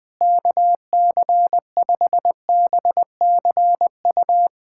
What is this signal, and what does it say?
Morse code, heard as a keyed tone, about 20 wpm, 700 Hz. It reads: KC5BCU
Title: USA Callsign DE Callsign - 6 Characters - 20wpm